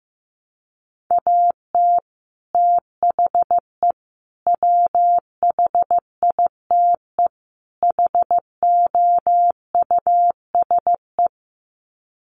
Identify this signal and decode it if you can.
Morse code, heard as a keyed tone, about 15 wpm, 700 Hz. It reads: AT THE WHITE HOUSE
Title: Sets of 4 Words 15wpm